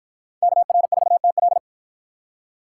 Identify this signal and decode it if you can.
Morse code, heard as a keyed tone, about 53 wpm, 700 Hz. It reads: KR4TL